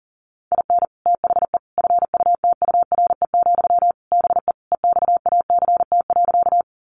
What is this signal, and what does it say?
Morse code, heard as a keyed tone, about 40 wpm, 700 Hz. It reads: IN THE FUTURE, BE EXACT.
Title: Sentences from Top 1300 Words 40wpm